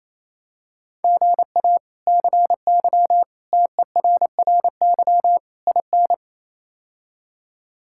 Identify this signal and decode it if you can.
Morse code, heard as a keyed tone, about 28 wpm, 700 Hz. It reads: GA CY TERRY IN
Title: K1USN SST QSO Practice - 28wpm